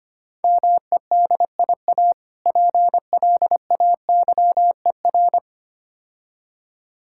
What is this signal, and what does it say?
Morse code, heard as a keyed tone, about 26 wpm, 700 Hz. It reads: MEDIA PLAYER
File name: practice / 25wpm / Difficult Words - Speed-Racing with Hints - 25wpm